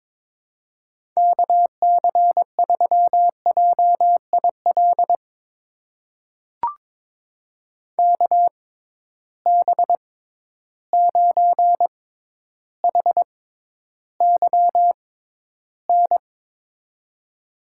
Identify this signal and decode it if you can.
Morse code, heard as a keyed tone, about 22 wpm, spaced as for 11 wpm, 700 Hz. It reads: KC3JIL KB9HYN
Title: USA Call Signs - 6 Characters - 6x Character Spacing - 22wpm